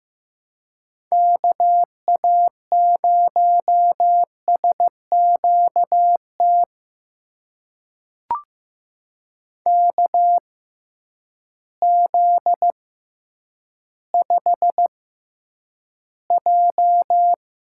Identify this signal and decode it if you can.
Morse code, heard as a keyed tone, about 16 wpm, spaced as for 9 wpm, 700 Hz. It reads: KA0SQT KZ5J
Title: USA Call Signs - 6 Characters - 6x Character Spacing - 15wpm